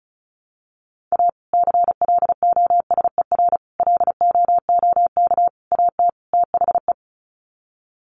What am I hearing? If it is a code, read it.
Morse code, heard as a keyed tone, about 34 wpm, 700 Hz. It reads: A CLOSER LOOK AT THE
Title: Sets of 5 Words Encore 35wpm